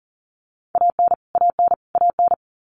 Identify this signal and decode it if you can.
Morse code, heard as a keyed tone, about 40 wpm, 700 Hz. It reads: AN AN AN
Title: Top 200 Words Repeated 40wpm